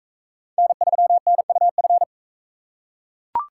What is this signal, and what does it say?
Morse code, heard as a keyed tone, about 39 wpm, 700 Hz. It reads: N3NUF